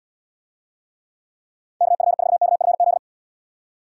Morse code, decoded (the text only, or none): KF4KFZ